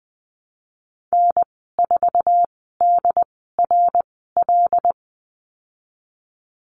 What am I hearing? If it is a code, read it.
Morse code, heard as a keyed tone, about 20 wpm, spaced as for 15 wpm, 700 Hz. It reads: N4DRL
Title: 082 Lesson 14g - Number 2 - Call Signs 20wpm